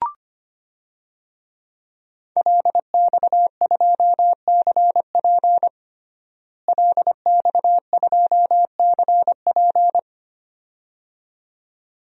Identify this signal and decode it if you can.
Morse code, heard as a keyed tone, about 25 wpm, 700 Hz. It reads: LX2CP LX2CP
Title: DXCC-100 Call Signs - 5 Characters - Repeated 2x - 25wpm